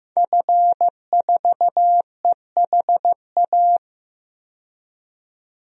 Morse code, heard as a keyed tone, about 15 wpm, 700 Hz. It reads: F4EHA
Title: DXCC-100 Call Signs - 5 Characters - 15wpm